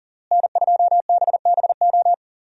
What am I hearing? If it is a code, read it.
Morse code, heard as a keyed tone, about 40 wpm, 700 Hz. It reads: N2BBO
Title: USA Call Signs - 5 Characters - Repeated 3x - 40wpm